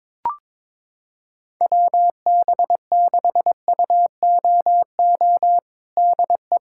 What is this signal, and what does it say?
Morse code, heard as a keyed tone, about 22 wpm, 700 Hz. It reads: WB6UOO DE
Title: USA Callsign DE Callsign 22wpm